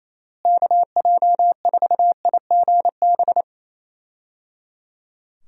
Morse code, heard as a keyed tone, about 28 wpm, 700 Hz. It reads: KJ4IGB
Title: USA Call Signs - 6 Characters - Repeated 3x - 28wpm